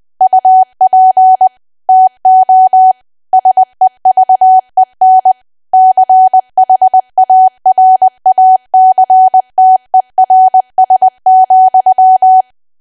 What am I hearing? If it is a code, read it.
Morse code, heard as a keyed tone, about 20 wpm, 750 Hz. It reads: UP TO SEVEN CHARACTERS,
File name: Archive / 20 WPM / 20WPM